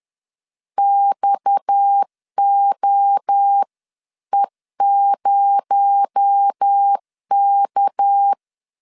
Morse code, heard as a keyed tone, about 10 wpm, 800 Hz. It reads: XOE0K
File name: CW / DET / 1 6 1500C S B 10wpm 4137